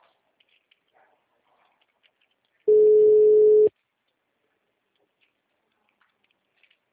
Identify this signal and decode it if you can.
Ringback tone (Europe, 425 Hz).